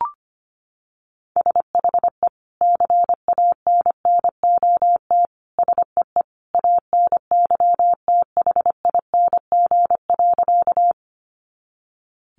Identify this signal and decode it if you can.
Morse code, heard as a keyed tone, about 25 wpm, 700 Hz. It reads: SHE CANNOT SEE ANYTHING.
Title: Additional Easy Phrases 25wpm